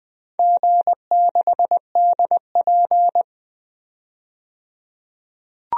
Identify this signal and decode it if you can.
Morse code, heard as a keyed tone, about 21 wpm, 700 Hz. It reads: G6DP
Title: DXCC-100 Call Signs - 4 Characters - 20wpm